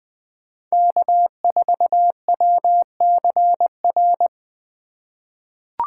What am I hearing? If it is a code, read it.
Morse code, heard as a keyed tone, about 21 wpm, 700 Hz. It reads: K4WCR
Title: USA Call Signs - 3x Character Spacing - 20wpm